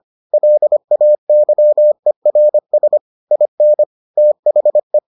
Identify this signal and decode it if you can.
Morse code, heard as a keyed tone, about 25 wpm, 600 Hz. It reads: LAYERS IN THE